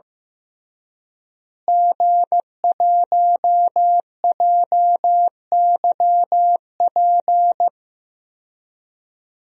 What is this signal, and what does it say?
Morse code, heard as a keyed tone, about 15 wpm, 700 Hz. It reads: G1JYP
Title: DXCC-100 Call Signs - 5 Characters - Repeated 3x - 15wpm